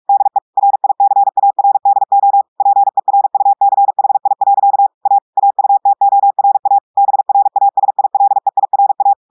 Morse code, heard as a keyed tone, about 46 wpm, 800 Hz. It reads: DE RICARDO PERUCHI. A AUTORA BRASILEIRA
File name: audios / portugais / portugais texte en clair 1 vitesse 45 mots minutes